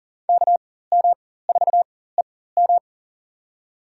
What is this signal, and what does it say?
Morse code, heard as a keyed tone, about 37 wpm, spaced as for 18 wpm, 700 Hz. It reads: KM4EM
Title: USA Call Signs - 5 Characters - 4x Character Spacing - 40wpm